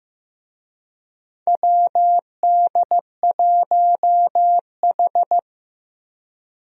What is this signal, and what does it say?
Morse code, heard as a keyed tone, about 16 wpm, 700 Hz. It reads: WD1H
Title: USA Call Signs - 4 Characters - Repeated 2x - 15wpm